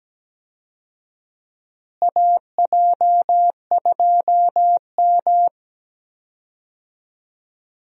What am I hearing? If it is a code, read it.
Morse code, heard as a keyed tone, about 17 wpm, 700 Hz. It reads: AJ2M
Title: USA Call Signs - 4 Characters - Repeated 3x - 17wpm